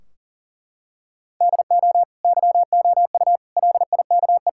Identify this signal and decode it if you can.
Morse code, heard as a keyed tone, about 40 wpm, 700 Hz. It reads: DO YOU LIKE
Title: Sentences from Top 1600 Words 40wpm